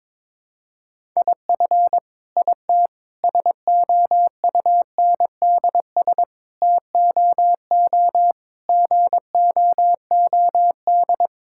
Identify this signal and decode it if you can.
Morse code, heard as a keyed tone, about 22 wpm, 700 Hz. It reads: IF IT SOUNDS TOO GOOD
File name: Sentences with 9-Word Phrases - 22wpm